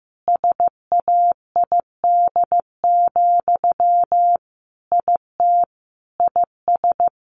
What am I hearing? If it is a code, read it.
Morse code, heard as a keyed tone, about 14 wpm, 700 Hz. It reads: SAID, IT IS
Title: Sentences from Top 300 Words 15wpm